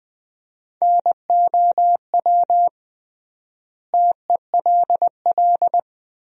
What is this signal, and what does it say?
Morse code, heard as a keyed tone, about 20 wpm, 700 Hz. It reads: NOW TELL
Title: 3x Word-Spacing - Sentences from Top 200 Words - 20wpm